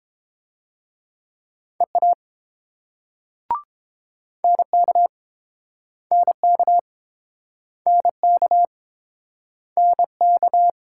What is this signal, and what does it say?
Morse code, heard as a keyed tone, about 30 wpm, 700 Hz. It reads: EA NK NK NK NK